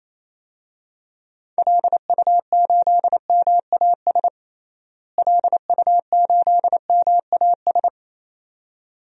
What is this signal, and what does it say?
Morse code, heard as a keyed tone, about 28 wpm, 700 Hz. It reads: LU8MAS LU8MAS
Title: DXCC-100 Call Signs - 6 Characters - Repeated 2x - 28wpm